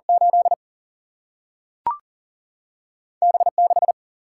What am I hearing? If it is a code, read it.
Morse code, heard as a keyed tone, about 40 wpm, 700 Hz. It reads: 8 B6
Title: Single-Letter Single-Number 40wpm